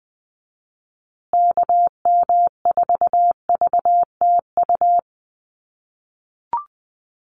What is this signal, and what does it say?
Morse code, heard as a keyed tone, about 20 wpm, 700 Hz. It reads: KM4VTU
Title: USA Call Signs Repeated 2x 20wpm